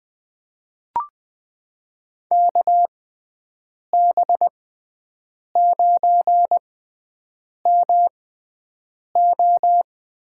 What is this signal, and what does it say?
Morse code, heard as a keyed tone, about 20 wpm, spaced as for 7 wpm, 700 Hz. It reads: KB9MO
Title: USA Call Signs - 6 Characters - 6x Character Spacing - 20wpm